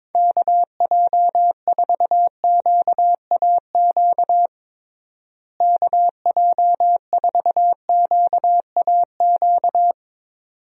Morse code, heard as a keyed tone, about 22 wpm, 700 Hz. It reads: KJ4QAQ KJ4QAQ
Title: USA Call Signs - 6 Characters - Repeated 2x - 22wpm